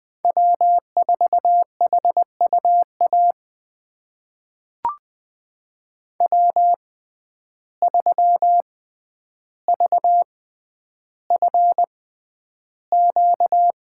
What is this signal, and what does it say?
Morse code, heard as a keyed tone, about 20 wpm, spaced as for 10 wpm, 700 Hz. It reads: W4HUA W3VFQ
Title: USA Call Signs - 6x Character Spacing - 20wpm